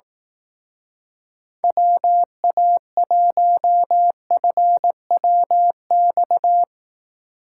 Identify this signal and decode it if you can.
Morse code, heard as a keyed tone, about 18 wpm, 700 Hz. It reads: WA1FWX
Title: Speed Racing - USA 2x3 Call Signs 15wpm